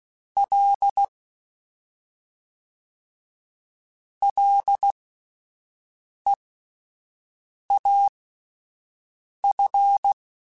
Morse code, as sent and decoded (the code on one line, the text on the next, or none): .-.. / .-.. .- ..-.
L LEAF